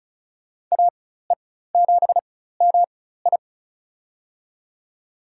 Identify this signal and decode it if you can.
Morse code, heard as a keyed tone, about 36 wpm, spaced as for 16 wpm, 700 Hz. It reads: AE7MI
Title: USA Call Signs - 5 Characters - 4x Character Spacing - 35wpm